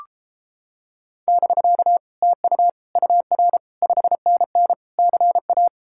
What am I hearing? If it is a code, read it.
Morse code, heard as a keyed tone, about 29 wpm, 700 Hz. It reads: <BK> TU UR 5NN CA